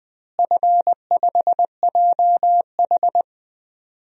Morse code, heard as a keyed tone, about 20 wpm, 700 Hz. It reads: F5JH
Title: DXCC-100 Call Signs - 4 Characters - Repeated 2x - 20wpm